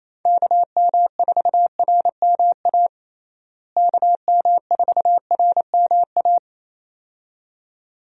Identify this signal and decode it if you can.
Morse code, heard as a keyed tone, about 28 wpm, 700 Hz. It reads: KM4RMA KM4RMA
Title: USA Call Signs - 6 Characters - Repeated 2x - 28wpm